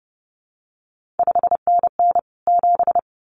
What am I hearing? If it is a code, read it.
Morse code, heard as a keyed tone, about 30 wpm, 700 Hz. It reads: 5NN 7
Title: CQ WW DX Contest Exchange - 30wpm